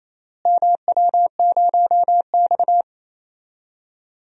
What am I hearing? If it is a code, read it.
Morse code, heard as a keyed tone, about 28 wpm, 700 Hz. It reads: MW0X
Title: DXCC-100 Call Signs - 4 Characters - Repeated 2x - 28wpm